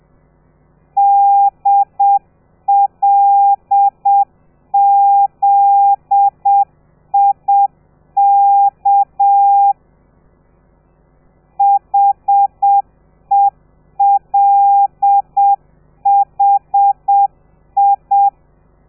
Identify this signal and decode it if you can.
Morse code, heard as a keyed tone, about 7 wpm, 800 Hz. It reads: DLZIK HELHI
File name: telegrafia / i2tfj / Lez13 30